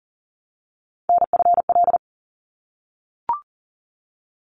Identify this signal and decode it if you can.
Morse code, heard as a keyed tone, about 40 wpm, 700 Hz. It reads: NFL
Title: ARRL Sections 40wpm